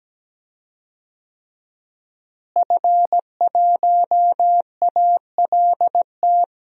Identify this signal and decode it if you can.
Morse code, heard as a keyed tone, about 17 wpm, 700 Hz. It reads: F1ALT
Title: DXCC-100 Call Signs - 5 Characters - Repeated 3x - 17wpm